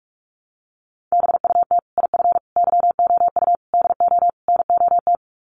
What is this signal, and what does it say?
Morse code, heard as a keyed tone, about 44 wpm, 700 Hz. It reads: BUT IF YOU DO NOT
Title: Sets of 5 Words Encore 45wpm